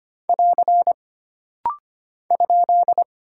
Morse code, heard as a keyed tone, about 25 wpm, 700 Hz. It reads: <AR> ?
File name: Single Punctuation-Prosign - Rapid-Fire - 25wpm